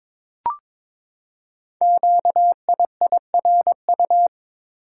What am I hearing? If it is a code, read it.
Morse code, heard as a keyed tone, about 23 wpm, 700 Hz. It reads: QIIRU
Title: Code Groups - 5 Letters - 22wpm